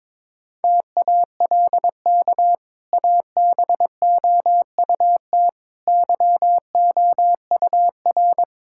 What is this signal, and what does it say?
Morse code, heard as a keyed tone, about 22 wpm, 700 Hz. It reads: TALK ABOUT YOUR
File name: practice / 22wpm / Easy Sentences with Hints - 22wpm